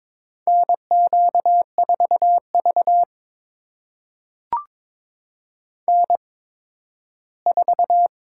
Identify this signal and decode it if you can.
Morse code, heard as a keyed tone, about 22 wpm, spaced as for 13 wpm, 700 Hz. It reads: NQ4V N4
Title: USA Call Signs - 4 Characters - 8x Character Spacing - 22wpm